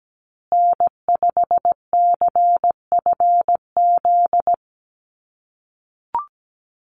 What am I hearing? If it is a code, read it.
Morse code, heard as a keyed tone, about 17 wpm, 700 Hz. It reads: N5CFZ
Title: USA Call Signs Repeated 2x 17wpm